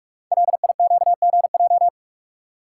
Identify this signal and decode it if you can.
Morse code, heard as a keyed tone, about 44 wpm, 690 Hz. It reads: LIQGJ
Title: Code Groups - 5 Letters - 45wpm